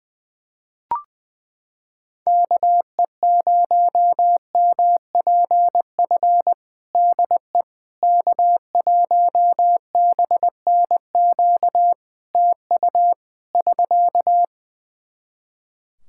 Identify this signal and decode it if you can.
Morse code, heard as a keyed tone, about 21 wpm, 700 Hz. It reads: KE0MPF DE K1BNQ TU <SK>
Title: Practice QSO - Exchange 4 - Ending - 20wpm